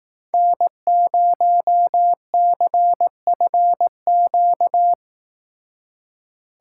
Morse code, heard as a keyed tone, about 18 wpm, 700 Hz. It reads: N0CFQ
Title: Speed Racing - USA 1x3 Call Signs 15wpm